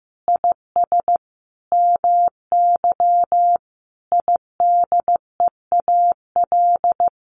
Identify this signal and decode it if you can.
Morse code, heard as a keyed tone, about 15 wpm, 700 Hz. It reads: IS MY IDEAL
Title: Sentences from Top 1400 Words 15wpm